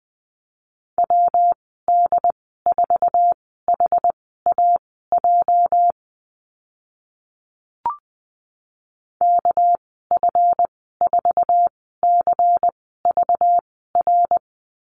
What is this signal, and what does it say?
Morse code, heard as a keyed tone, about 19 wpm, spaced as for 14 wpm, 700 Hz. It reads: WD4HAJ KF4CVR
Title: 213 Lesson 34g - Number 8 - Call Signs 20wpm